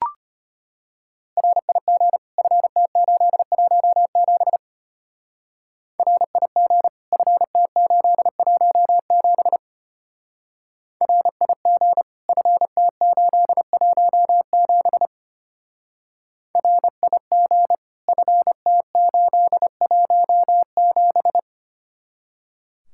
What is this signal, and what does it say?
Morse code, heard as a keyed tone, about 32 wpm, 700 Hz. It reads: RIG FT817 RIG FT817 RIG FT817 RIG FT817